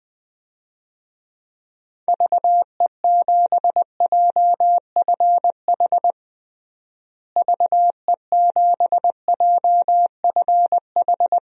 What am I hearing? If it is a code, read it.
Morse code, heard as a keyed tone, about 20 wpm, 700 Hz. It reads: VE7JFH VE7JFH
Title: DXCC-100 Call Signs - 6 Characters - Repeated 2x - 20wpm